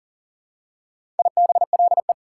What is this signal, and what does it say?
Morse code, heard as a keyed tone, about 40 wpm, 700 Hz. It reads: IBLE
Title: Prefixes and Suffixes 40wpm